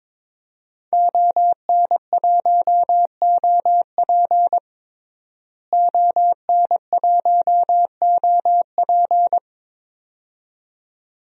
Morse code, heard as a keyed tone, about 22 wpm, 700 Hz. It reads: ON1OP ON1OP
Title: DXCC-100 Call Signs - 5 Characters - Repeated 2x - 22wpm